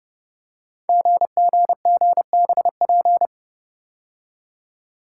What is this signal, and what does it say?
Morse code, heard as a keyed tone, about 29 wpm, 700 Hz. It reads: GGGBP
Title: Code Groups - 5 Letters - 30wpm